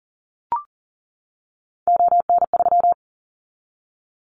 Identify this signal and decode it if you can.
Morse code, heard as a keyed tone, about 41 wpm, 700 Hz. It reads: ON3
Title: Callsign Prefixes for Top 25 DXCC - 40wpm